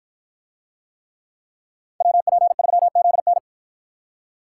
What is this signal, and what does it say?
Morse code, heard as a keyed tone, about 53 wpm, 700 Hz. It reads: WP3ZN